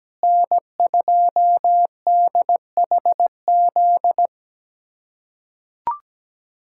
Morse code, heard as a keyed tone, about 17 wpm, 700 Hz. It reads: N2DHZ
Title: USA Callsign DE Callsign - 5 Characters - 17wpm